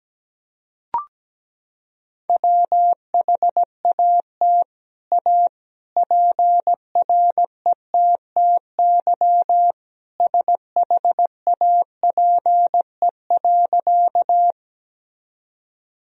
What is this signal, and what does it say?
Morse code, heard as a keyed tone, about 17 wpm, 700 Hz. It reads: WHAT A PRETTY SHAPE.
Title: Sentences from Top 1000 Words - Review Part 3 - 17wpm